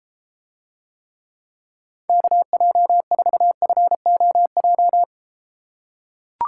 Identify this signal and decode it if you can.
Morse code, heard as a keyed tone, about 32 wpm, 700 Hz. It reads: KJ4FOJ